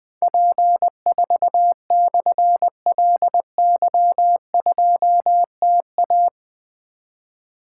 Morse code, heard as a keyed tone, about 20 wpm, 700 Hz. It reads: P4/LY2TA
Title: Difficult Call Signs 20wpm